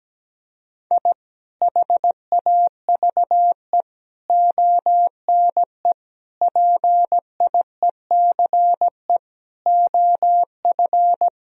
Morse code, heard as a keyed tone, about 17 wpm, 700 Hz. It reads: I HAVE ONE PIECE OF
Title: Sentences from Top 1000 Words - Review Part 2 - 17wpm